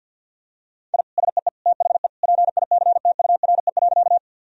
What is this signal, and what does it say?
Morse code, heard as a keyed tone, about 47 wpm, 700 Hz. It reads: I SEE THE PICTURE.
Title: Sentences from Top 1000 Words - Review Part 1 - 50wpm